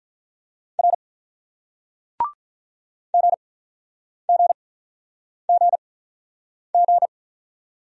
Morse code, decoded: R G G G G

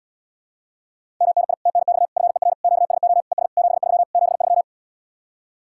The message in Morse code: MNI TNX FER QSO ES 73 73

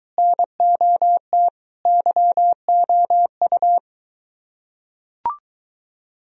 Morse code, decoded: NOT YOU

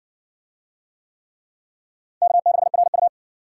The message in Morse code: K6RU

